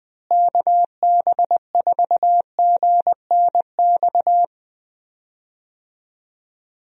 Morse code, heard as a keyed tone, about 21 wpm, 700 Hz. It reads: KB4GNX
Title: USA Call Signs - 6 Characters - Repeated 2x - 20wpm